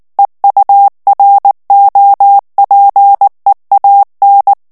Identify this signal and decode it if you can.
Morse code, heard as a keyed tone, about 19 wpm, 800 Hz. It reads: EUROPEAN
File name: ANGLAIS texte clair 2 vitesse 18 mots minute